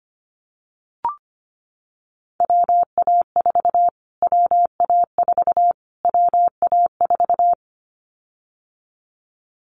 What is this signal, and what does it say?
Morse code, heard as a keyed tone, about 25 wpm, 700 Hz. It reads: WA4 WA4 WA4
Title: Callsign Prefixes for Top 25 DXCC - Repeated 3x - 25wpm